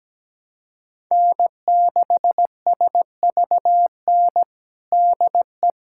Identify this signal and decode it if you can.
Morse code, heard as a keyed tone, about 17 wpm, 700 Hz. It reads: N6SVN DE
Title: USA Callsign DE Callsign - 5 Characters - 17wpm